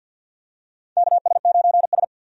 Morse code, heard as a keyed tone, about 53 wpm, 700 Hz. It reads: KS9S
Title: USA Call Signs - 4 Characters - 50wpm